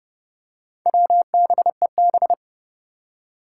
Morse code, heard as a keyed tone, about 30 wpm, 700 Hz. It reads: WBEB
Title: Four Challenging Letter-Number Combinations 30wpm